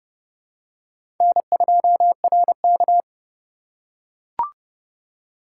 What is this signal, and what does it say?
Morse code, heard as a keyed tone, about 30 wpm, 700 Hz. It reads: N2RK
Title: USA Call Signs - 4 Characters - 4x Character Spacing - 30wpm